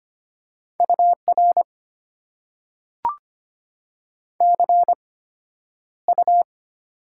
Morse code, heard as a keyed tone, about 25 wpm, spaced as for 11 wpm, 700 Hz. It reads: UR CU
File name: Top 90 QSO Elements - 8x Character Spacing - 25wpm